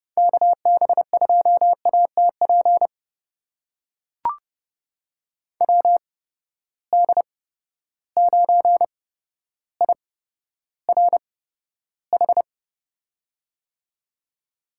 Morse code, heard as a keyed tone, about 30 wpm, spaced as for 13 wpm, 700 Hz. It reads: KB2ATP WD9IRH
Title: USA Call Signs - 6 Characters - 8x Character Spacing - 30wpm